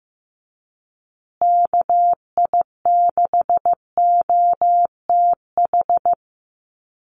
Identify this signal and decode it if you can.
Morse code, heard as a keyed tone, about 15 wpm, 700 Hz. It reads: KI6OTH
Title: USA Call Signs Repeated 2x 15wpm